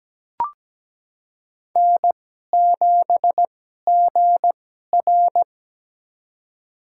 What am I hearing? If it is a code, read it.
Morse code, heard as a keyed tone, about 17 wpm, spaced as for 12 wpm, 700 Hz. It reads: N7GR
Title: USA Call Signs - 4 Characters - 2x Character Spacing - 17wpm